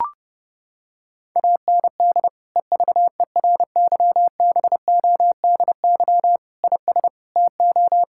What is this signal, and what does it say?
Morse code, heard as a keyed tone, about 32 wpm, 700 Hz. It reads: AND EVERYBODY IS TO
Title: Sentences from Top 1600 Words 30wpm